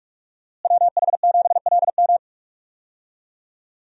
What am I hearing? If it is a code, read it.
Morse code, heard as a keyed tone, about 41 wpm, 700 Hz. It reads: WH7LM
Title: USA Call Signs - 5 Characters - Repeated 3x - 45wpm